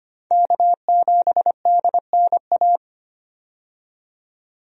Morse code, heard as a keyed tone, about 25 wpm, 700 Hz. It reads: K7DNA
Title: USA Call Signs - 5 Characters - 25wpm